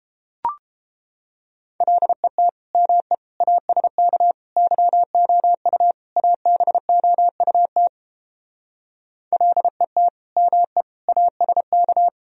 Morse code, heard as a keyed tone, about 31 wpm, 700 Hz. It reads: LET ME ASK YOU ABOUT LET ME ASK